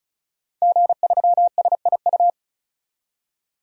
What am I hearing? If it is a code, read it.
Morse code, heard as a keyed tone, about 36 wpm, 700 Hz. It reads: G3SIU
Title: DXCC-100 Call Signs - 5 Characters - 35wpm